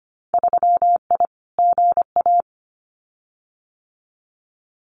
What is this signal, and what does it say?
Morse code, heard as a keyed tone, about 25 wpm, 700 Hz. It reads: 3I GA
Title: Winter Field Day Exchanges 25wpm